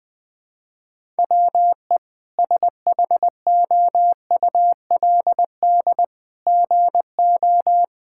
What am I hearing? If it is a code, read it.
Morse code, heard as a keyed tone, about 20 wpm, 700 Hz. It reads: WE SHOULD GO
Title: Sentences with 3-Word Phrases - Spoken Beginning - 20wpm